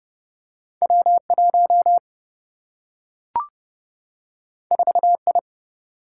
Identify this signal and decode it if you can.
Morse code, heard as a keyed tone, about 30 wpm, 700 Hz. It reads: W 1 4 I
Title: Single-Letter Single-Number 30wpm